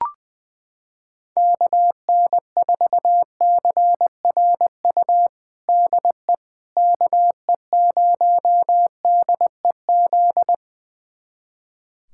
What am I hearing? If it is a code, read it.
Morse code, heard as a keyed tone, about 20 wpm, 700 Hz. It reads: KN4CRU DE KE0DEZ
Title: USA Callsign DE Callsign - 6 Characters - 20wpm